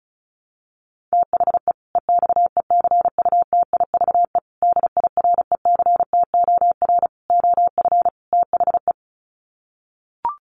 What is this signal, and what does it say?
Morse code, heard as a keyed tone, about 34 wpm, 700 Hz. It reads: THE EXECUTIVE DIRECTOR OF THE
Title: Sets of 5 Words Encore 35wpm